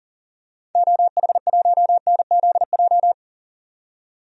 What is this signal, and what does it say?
Morse code, heard as a keyed tone, about 40 wpm, 700 Hz. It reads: OH1NZJ
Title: DXCC-100 Call Signs - 6 Characters - 40wpm